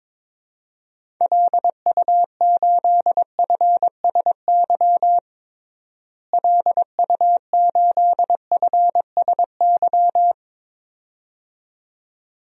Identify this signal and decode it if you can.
Morse code, heard as a keyed tone, about 22 wpm, 700 Hz. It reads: LU8FSY LU8FSY
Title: DXCC-100 Call Signs - 6 Characters - Repeated 2x - 22wpm